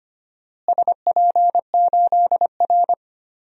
Morse code, heard as a keyed tone, about 25 wpm, 700 Hz. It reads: SP8R
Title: DXCC-100 Call Signs - 4 Characters - Repeated 3x - 25wpm